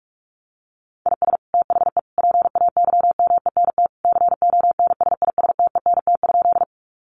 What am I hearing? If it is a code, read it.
Morse code, heard as a keyed tone, about 44 wpm, 700 Hz. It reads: IS THE PAYMENT CONSISTENT?
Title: Sentences from Top 1500 Words 45wpm